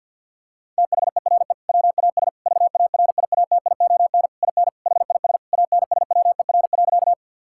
Morse code, heard as a keyed tone, about 50 wpm, 700 Hz. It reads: THERE WAS VARIATION IN HIS ANSWER.
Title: Sentences from Top 1600 Words 50wpm